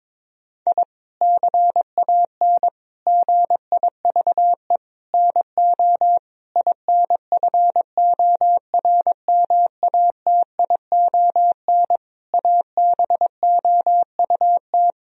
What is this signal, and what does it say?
Morse code, heard as a keyed tone, about 22 wpm, 700 Hz. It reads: I CAN GIVE NO INFORMATION ABOUT
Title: Sentences from Top 1000 Words - Review Part 3 - 22wpm